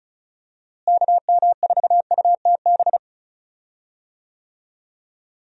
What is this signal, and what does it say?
Morse code, heard as a keyed tone, about 36 wpm, 700 Hz. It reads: KM4UTB
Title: USA Call Signs - 6 Characters - Repeated 2x - 35wpm